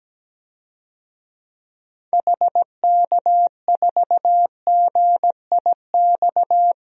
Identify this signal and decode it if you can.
Morse code, heard as a keyed tone, about 17 wpm, 700 Hz. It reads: HK4GIX
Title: DXCC-100 Call Signs - 6 Characters - 17wpm